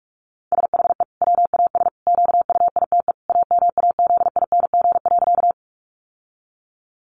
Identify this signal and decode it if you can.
Morse code, heard as a keyed tone, about 44 wpm, 700 Hz. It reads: SHE WAS QUITE AMAZING.
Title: Sentences from Top 1500 Words 45wpm